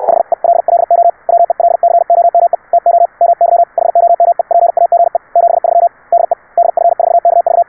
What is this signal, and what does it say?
Morse code, heard as a keyed tone, about 69 wpm, spaced as for 60 wpm, 690 Hz. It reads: HELLO WELCOME TO MY HOMEPAGE 73 DE DF4KV